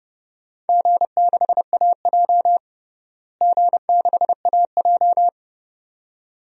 Morse code, heard as a keyed tone, about 30 wpm, 700 Hz. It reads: G6AJ G6AJ
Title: DXCC-100 Call Signs - 4 Characters - Repeated 2x - 30wpm